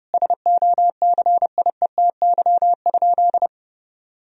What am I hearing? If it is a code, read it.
Morse code, heard as a keyed tone, about 30 wpm, 700 Hz. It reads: SOCIETY?
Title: Sentences from Top 1000 Words - Review Part 4 - 30wpm